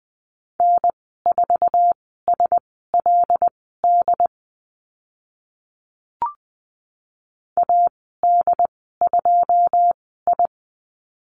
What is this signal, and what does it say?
Morse code, heard as a keyed tone, about 20 wpm, spaced as for 14 wpm, 700 Hz. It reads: N4SLD AD2I
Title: 082 Lesson 14g - Number 2 - Call Signs 20wpm